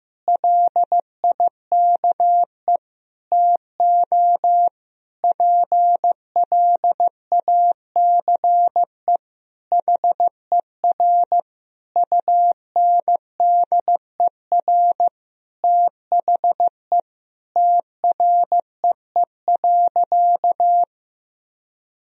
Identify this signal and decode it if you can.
Morse code, heard as a keyed tone, about 15 wpm, 700 Hz. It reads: LIKE TO PLACE HER UNDER THE TREE.
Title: Sentences from Top 1000 Words - Review Part 1 - 15wpm